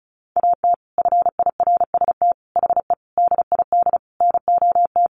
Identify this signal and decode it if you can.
Morse code, heard as a keyed tone, about 34 wpm, 700 Hz. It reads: AT FIRST HE DID NOT
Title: Sentences from Top 600 Words 35wpm